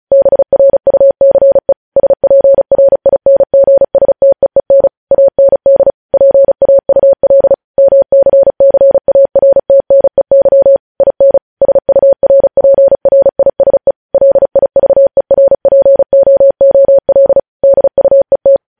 Morse code, heard as a keyed tone, about 36 wpm, 550 Hz. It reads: BRUCE SPRINGSTEEN AND PAUL MCCARTNEY IN SURPRISE LIVERPOOL DUET
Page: news headlines in morse code at 35 wpm